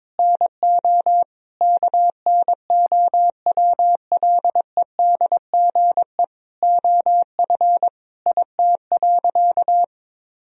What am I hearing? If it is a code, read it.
Morse code, heard as a keyed tone, about 22 wpm, 700 Hz. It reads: NO KNOWLEDGE OF IT.
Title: Sentences from Top 1000 Words - Review Part 3 - 22wpm